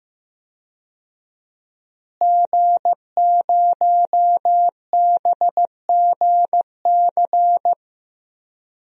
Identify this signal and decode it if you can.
Morse code, heard as a keyed tone, about 15 wpm, 700 Hz. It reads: G0BGC
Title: DXCC-100 Call Signs - 5 Characters - Repeated 2x - 15wpm